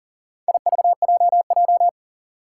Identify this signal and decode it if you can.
Morse code, heard as a keyed tone, about 40 wpm, 690 Hz. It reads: IVJJ
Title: Four Challenging Letter-Number Combinations 40wpm